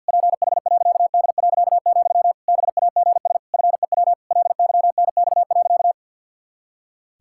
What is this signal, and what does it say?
Morse code, heard as a keyed tone, about 47 wpm, 700 Hz. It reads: PH.D., BAGS FEW LYNX.
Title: Pangram Challenge v2 - 50wpm